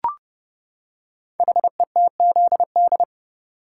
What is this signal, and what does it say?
Morse code, heard as a keyed tone, about 28 wpm, 700 Hz. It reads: HETZD
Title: Code Groups - 5 Letters - 30wpm